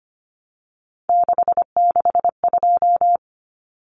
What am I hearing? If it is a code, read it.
Morse code, heard as a keyed tone, about 25 wpm, 700 Hz. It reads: 662
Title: Numbers - 3-Digit 25wpm